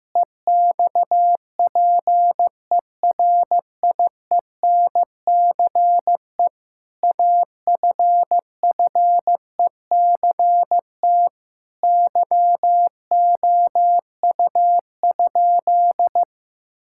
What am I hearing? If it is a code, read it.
Morse code, heard as a keyed tone, about 16 wpm, 700 Hz. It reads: EXPERIENCE AFFECT YOU?
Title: Sentences from Top 1000 Words - Review Part 6 - 15wpm